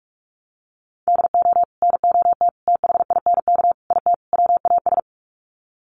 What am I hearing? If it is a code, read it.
Morse code, heard as a keyed tone, about 44 wpm, 700 Hz. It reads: DO NOT THINK IT WAS
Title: Sets of 5 Words Encore 45wpm